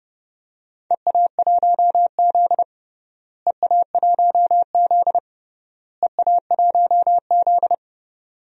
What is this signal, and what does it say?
Morse code, heard as a keyed tone, about 30 wpm, 700 Hz. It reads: EA1Z EA1Z EA1Z
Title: DXCC-100 Call Signs - 4 Characters - Repeated 3x - 30wpm